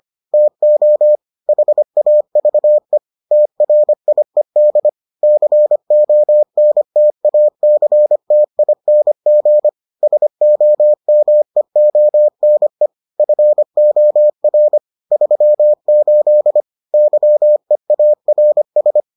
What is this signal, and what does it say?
Morse code, heard as a keyed tone, about 25 wpm, 600 Hz. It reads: TO HAVE TRIED CONTACTING SOMEONE FOR 38 YEARS